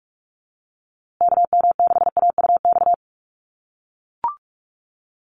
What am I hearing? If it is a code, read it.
Morse code, heard as a keyed tone, about 44 wpm, 700 Hz. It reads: KM6AUX
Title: USA Call Signs Repeated 2x 45wpm